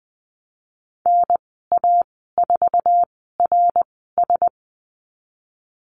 Morse code, heard as a keyed tone, about 20 wpm, spaced as for 15 wpm, 700 Hz. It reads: NA4RS
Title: 088 Lesson 15g - Number 5 - Call Signs 20wpm